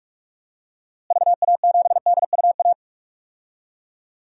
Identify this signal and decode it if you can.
Morse code, heard as a keyed tone, about 42 wpm, 700 Hz. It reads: VA7DUA
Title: DXCC-100 Call Signs - 6 Characters - Repeated 2x - 45wpm